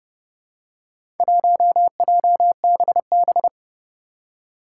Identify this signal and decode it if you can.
Morse code, heard as a keyed tone, about 30 wpm, 700 Hz. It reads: 1JBB
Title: Four Challenging Letter-Number Combinations 30wpm